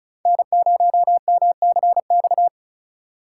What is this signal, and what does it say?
Morse code, heard as a keyed tone, about 34 wpm, 700 Hz. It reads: N0MCX
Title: USA Call Signs - 5 Characters - 2x Character Spacing - 35wpm